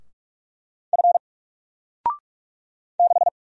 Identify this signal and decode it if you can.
Morse code, heard as a keyed tone, about 44 wpm, 700 Hz. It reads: F 6
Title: Single Letter-Number Alt 45wpm